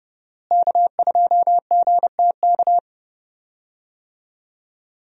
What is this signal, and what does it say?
Morse code, heard as a keyed tone, about 30 wpm, 700 Hz. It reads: K2GTK
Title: USA Call Signs - 5 Characters - Repeated 3x - 30wpm